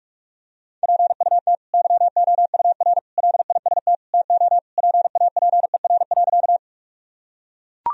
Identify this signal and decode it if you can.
Morse code, heard as a keyed tone, about 41 wpm, 700 Hz. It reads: PUT YOUR LIST TO PAPER.
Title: Sentences from Top 1000 Words - Review Part 2 - 45wpm